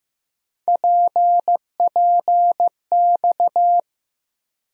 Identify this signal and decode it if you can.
Morse code, heard as a keyed tone, about 15 wpm, 700 Hz. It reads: PPX
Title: USA Call Sign Postfixes - 3 Letters 15wpm